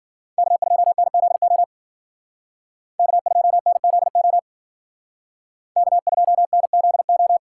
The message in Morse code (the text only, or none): K2NZO K2NZO K2NZO